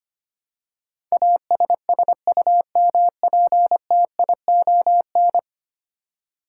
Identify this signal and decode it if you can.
Morse code, heard as a keyed tone, about 25 wpm, 700 Hz. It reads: ASSUMPTION
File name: 10 Letter Words - First 3 Letters Given - 25wpm